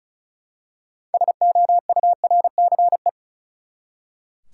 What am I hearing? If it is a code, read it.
Morse code, heard as a keyed tone, about 36 wpm, 700 Hz. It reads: SOURCE
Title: Six Letter Words - 35wpm